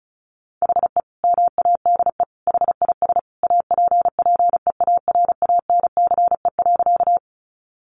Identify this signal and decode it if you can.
Morse code, heard as a keyed tone, about 34 wpm, 700 Hz. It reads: HE MADE HIS APPEARANCE.
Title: Sentences from Top 1100 Words 35wpm